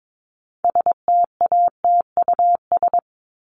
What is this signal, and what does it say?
Morse code, heard as a keyed tone, about 22 wpm, 700 Hz. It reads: STATUS
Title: Top 1251-1500 Words 22wpm